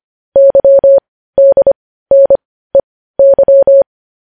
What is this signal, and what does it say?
Morse code, heard as a keyed tone, about 25 wpm, spaced as for 15 wpm, 550 Hz. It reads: YDNEY